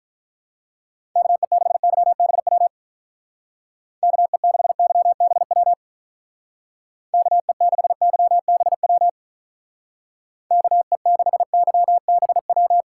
Morse code, heard as a keyed tone, about 45 wpm, 700 Hz. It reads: KE6YBW KE6YBW KE6YBW KE6YBW